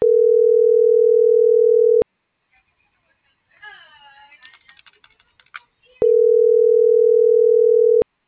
Ringback tone (North America), 440 + 480 Hz.
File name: CENTRAL TERRESTRE / out karis